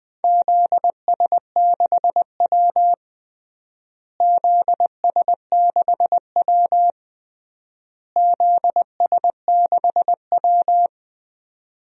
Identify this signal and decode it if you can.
Morse code, heard as a keyed tone, about 21 wpm, 700 Hz. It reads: ZS6W ZS6W ZS6W
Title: DXCC-100 Call Signs - 4 Characters - Repeated 3x - 20wpm